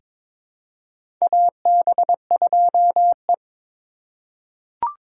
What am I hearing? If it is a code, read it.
Morse code, heard as a keyed tone, about 21 wpm, 690 Hz. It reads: AB2E
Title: USA Call Signs - 4 Characters - 3x Character Spacing - 22wpm